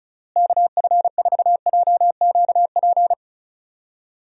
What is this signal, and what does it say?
Morse code, heard as a keyed tone, about 36 wpm, 700 Hz. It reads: KF4JQP
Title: USA Call Signs - 6 Characters - Repeated 2x - 35wpm